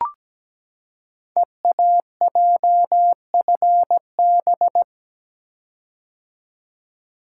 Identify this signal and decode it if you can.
Morse code, heard as a keyed tone, about 17 wpm, 700 Hz. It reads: EAJFB
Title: Code Groups - 5 Letter:Numbers - 17wpm